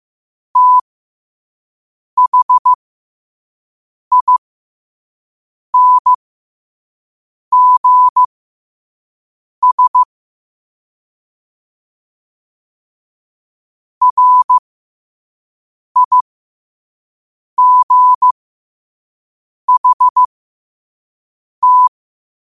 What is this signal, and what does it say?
Morse code, heard as a keyed tone, about 15 wpm, spaced as for 5 wpm, 1000 Hz. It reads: THINGS RIGHT
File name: qotdfiles / 5wpm